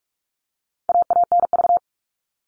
Morse code, heard as a keyed tone, about 44 wpm, 700 Hz. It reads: AANV
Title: All Cut Numbers - 4-Digit 45wpm